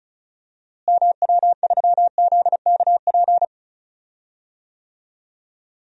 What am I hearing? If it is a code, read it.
Morse code, heard as a keyed tone, about 34 wpm, 700 Hz. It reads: MW3ZKP
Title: DXCC-100 Call Signs - 6 Characters - Repeated 2x - 35wpm